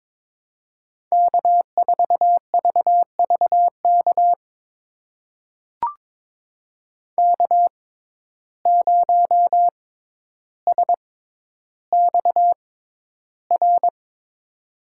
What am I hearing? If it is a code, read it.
Morse code, heard as a keyed tone, about 22 wpm, spaced as for 11 wpm, 700 Hz. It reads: K4VVK K0SXR
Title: USA Call Signs - 5 Characters - 6x Character Spacing - 22wpm